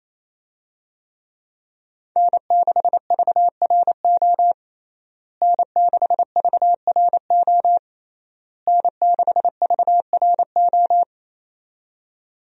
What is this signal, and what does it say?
Morse code, heard as a keyed tone, about 28 wpm, 700 Hz. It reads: N6VRO N6VRO N6VRO
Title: USA Call Signs - 5 Characters - Repeated 3x - 28wpm